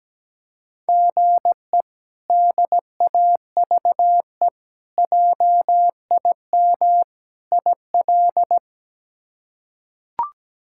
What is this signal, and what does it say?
Morse code, heard as a keyed tone, about 17 wpm, 700 Hz. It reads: GE DAVE JIM IL
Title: K1USN SST QSO Practice - 17wpm